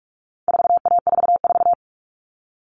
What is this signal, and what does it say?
Morse code, heard as a keyed tone, about 44 wpm, 700 Hz. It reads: 4A44
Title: All Cut Numbers - 4-Digit 45wpm